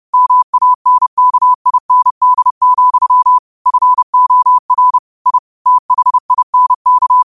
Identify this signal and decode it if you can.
Morse code, heard as a keyed tone, about 30 wpm, 1000 Hz. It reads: MANKIND, FOR I THINK
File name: qotdfiles / 30wpm